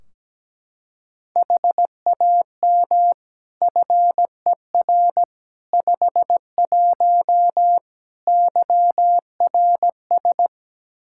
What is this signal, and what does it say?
Morse code, heard as a keyed tone, about 17 wpm, 700 Hz. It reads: HAM FER 51 YRS
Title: Practice QSO - Exchange 3 - More Info - 17wpm